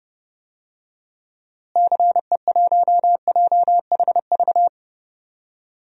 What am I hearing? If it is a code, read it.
Morse code, heard as a keyed tone, about 30 wpm, 700 Hz. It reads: CE1JHV
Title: DXCC-100 Call Signs - 6 Characters - Repeated 3x - 30wpm